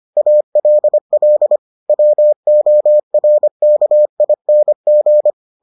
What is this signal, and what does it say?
Morse code, heard as a keyed tone, about 25 wpm, 600 Hz. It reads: ALL WORKING